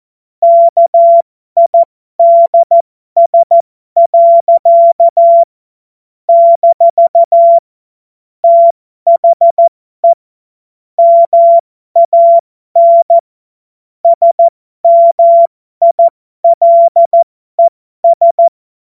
Morse code, heard as a keyed tone, about 14 wpm, spaced as for 12 wpm, 680 Hz. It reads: KIDS. - THE MAN SMILES
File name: Morsekurs / Witze